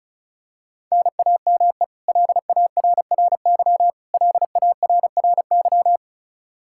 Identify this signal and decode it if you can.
Morse code, heard as a keyed tone, about 36 wpm, 700 Hz. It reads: NAME LARRY LARRY
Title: Practice QSO - Exchange 1 - Beginning - 35wpm